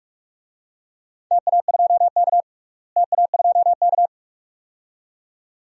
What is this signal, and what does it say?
Morse code, heard as a keyed tone, about 42 wpm, 700 Hz. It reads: TA2K TA2K
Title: DXCC-100 Call Signs - 4 Characters - Repeated 2x - 45wpm